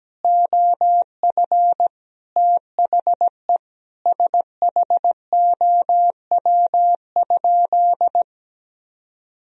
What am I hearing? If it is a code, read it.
Morse code, heard as a keyed tone, about 17 wpm, 700 Hz. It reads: OF THE SHOW?
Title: Sentences from Top 1000 Words - Review Part 2 - 17wpm